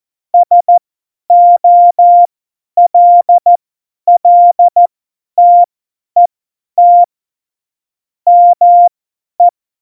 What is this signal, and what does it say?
Morse code, heard as a keyed tone, about 14 wpm, spaced as for 10 wpm, 700 Hz. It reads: SOLLTET ME
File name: Morsekurs / Witze / Witz 01. Dezember 2020, 50 F70